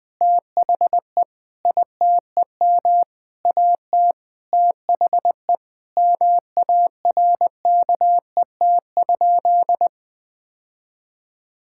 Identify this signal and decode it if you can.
Morse code, heard as a keyed tone, about 20 wpm, 700 Hz. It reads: THE ITEM AT THE MARKET?
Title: Sentences from Top 1000 Words - Review Part 6 - 20wpm